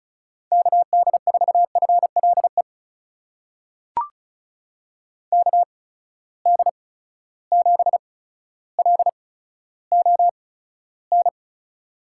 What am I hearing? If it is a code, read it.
Morse code, heard as a keyed tone, about 36 wpm, spaced as for 15 wpm, 700 Hz. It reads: KD4FLE KD7LON
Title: USA Call Signs - 6 Characters - 8x Character Spacing - 35wpm